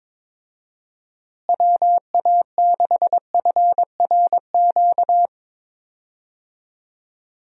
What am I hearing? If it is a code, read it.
Morse code, heard as a keyed tone, about 22 wpm, 700 Hz. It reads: WA6FRQ
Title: USA Call Signs - 6 Characters - Repeated 2x - 22wpm